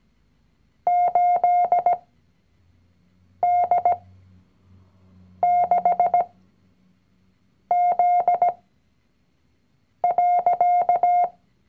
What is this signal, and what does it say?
Morse code, heard as a keyed tone, about 17 wpm, spaced as for 5 wpm, 710 Hz. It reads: 8D6Z.
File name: CW-Online-Kurs / Anfänger / Lektion 019